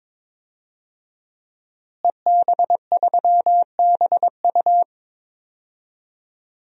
Morse code, heard as a keyed tone, about 22 wpm, 700 Hz. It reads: EB3BU
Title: DXCC-100 Call Signs - 5 Characters - Repeated 2x - 22wpm